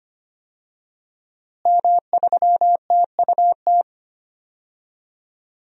Morse code, heard as a keyed tone, about 25 wpm, 700 Hz. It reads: M3TUT
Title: DXCC-100 Call Signs - 5 Characters - Repeated 2x - 25wpm